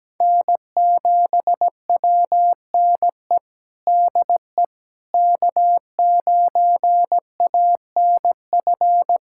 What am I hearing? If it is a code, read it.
Morse code, heard as a keyed tone, about 17 wpm, 700 Hz. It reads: N7WNE DE K9ANF
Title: USA Callsign DE Callsign - 5 Characters - 17wpm